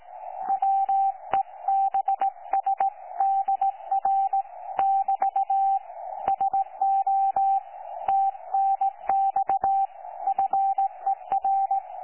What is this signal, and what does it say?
Morse code, heard as a keyed tone, about 17 wpm, 780 Hz. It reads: WEBSDR <BT> SO TNX FER
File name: 12 39Z 18088.2kHz utah aa7fv k9tjl